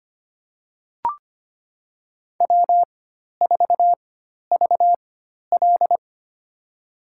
Morse code, heard as a keyed tone, about 25 wpm, spaced as for 12 wpm, 700 Hz. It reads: W4VL
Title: USA Call Signs - 4 Characters - 4x Character Spacing - 25wpm